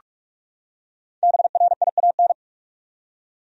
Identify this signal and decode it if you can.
Morse code, heard as a keyed tone, about 42 wpm, 700 Hz. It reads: BRIAN
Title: USA Common Names - Max 5 Letters - 45wpm